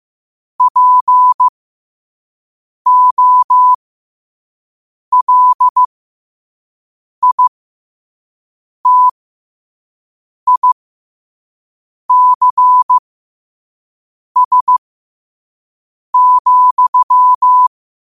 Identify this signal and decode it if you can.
Morse code, heard as a keyed tone, about 15 wpm, spaced as for 5 wpm, 1000 Hz. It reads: POLITICS,
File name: qotdfiles / 5wpm